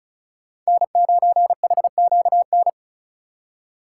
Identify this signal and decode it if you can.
Morse code, heard as a keyed tone, about 36 wpm, 690 Hz. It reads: N9HQN
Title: USA Call Signs - 5 Characters - 3x Character Spacing - 35wpm